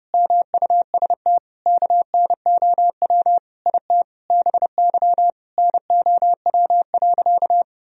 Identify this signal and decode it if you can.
Morse code, heard as a keyed tone, about 30 wpm, 700 Hz. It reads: MUST KNOW IT BY NOW.
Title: Sentences from Top 1000 Words - Review Part 2 - 30wpm